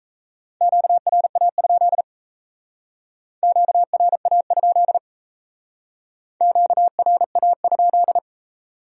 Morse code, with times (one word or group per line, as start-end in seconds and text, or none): 0.6-2.0 QRA?
3.4-5.0 QRA?
6.4-8.2 QRA?